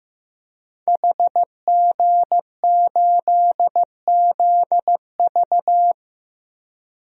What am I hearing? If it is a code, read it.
Morse code, heard as a keyed tone, about 15 wpm, 700 Hz. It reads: HG8ZV
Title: DXCC-100 Call Signs - 5 Characters - Repeated 3x - 15wpm